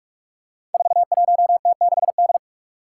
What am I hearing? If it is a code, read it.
Morse code, heard as a keyed tone, about 46 wpm, 700 Hz. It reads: 41T6D
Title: Five Challenging Letter-Number Combinations 45wpm